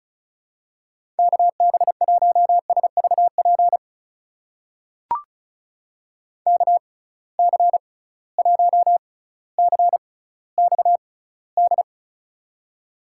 Morse code, heard as a keyed tone, about 36 wpm, spaced as for 18 wpm, 700 Hz. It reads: KB1SVP KC1CXD
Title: USA Call Signs - 6 Characters - 6x Character Spacing - 35wpm